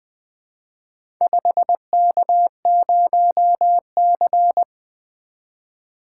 Morse code, heard as a keyed tone, about 20 wpm, 700 Hz. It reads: 5K0C
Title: Four Letter-Number Combinations - 20wpm